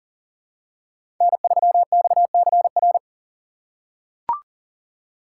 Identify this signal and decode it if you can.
Morse code, heard as a keyed tone, about 40 wpm, 700 Hz. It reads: N3XCR
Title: USA Call Signs - 5 Characters - 3x Character Spacing - 40wpm